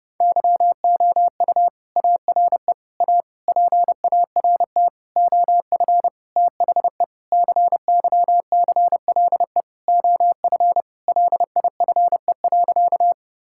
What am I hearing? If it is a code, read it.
Morse code, heard as a keyed tone, about 30 wpm, 700 Hz. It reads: YOU ARE A PART OF THE CYCLE OF LIFE.
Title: Sentences from Top 1000 Words - Review Part 6 - 30wpm